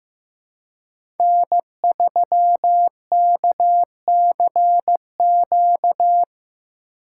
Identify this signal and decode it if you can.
Morse code, heard as a keyed tone, about 16 wpm, 700 Hz. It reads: N3KCQ
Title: USA Call Signs - 5 Characters - 4x Character Spacing - 15wpm